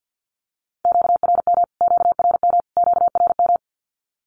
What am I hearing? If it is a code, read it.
Morse code, heard as a keyed tone, about 50 wpm, 700 Hz. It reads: QRM QRM QRM
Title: Q-Codes Repeated 3x 50wpm